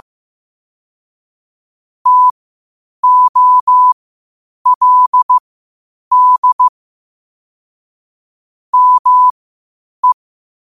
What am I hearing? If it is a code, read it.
Morse code, heard as a keyed tone, about 15 wpm, spaced as for 8 wpm, 1000 Hz. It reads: TOLD ME
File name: qotdfiles / 8wpm